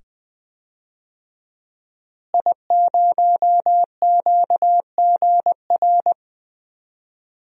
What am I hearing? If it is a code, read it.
Morse code, heard as a keyed tone, about 20 wpm, 700 Hz. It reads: I0QGR
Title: DXCC-100 Call Signs - 5 Characters - Repeated 2x - 20wpm